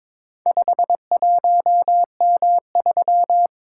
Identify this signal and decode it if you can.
Morse code, heard as a keyed tone, about 22 wpm, 700 Hz. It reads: 51M3
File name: practice / 22wpm / Four Letter-Number Combinations - 22wpm